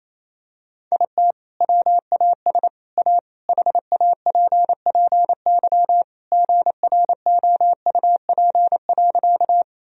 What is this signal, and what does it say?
Morse code, heard as a keyed tone, about 28 wpm, 700 Hz. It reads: IT WAS A HAPPY GROUP.
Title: Sentences from Top 1000 Words - Review Part 1 - 28wpm